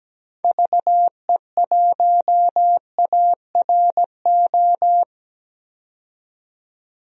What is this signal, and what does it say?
Morse code, heard as a keyed tone, about 17 wpm, 700 Hz. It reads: VE1ARO
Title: DXCC-100 Call Signs - 6 Characters - 17wpm